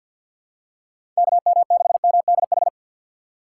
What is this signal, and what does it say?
Morse code, heard as a keyed tone, about 50 wpm, 700 Hz. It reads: KM6MDH
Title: USA Call Signs - 6 Characters - 6x Character Spacing - 50wpm